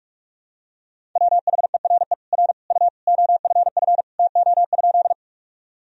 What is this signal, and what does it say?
Morse code, heard as a keyed tone, about 47 wpm, 700 Hz. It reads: WHERE R U OFF TO?
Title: Sentences from Top 200 Words and Abbreviations 45wpm